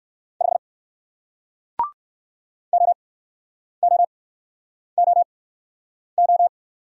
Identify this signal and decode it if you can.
Morse code, heard as a keyed tone, about 55 wpm, spaced as for 9 wpm, 700 Hz. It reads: 5 OOOO
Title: Speed-Racing Single Letter-Number 45wpm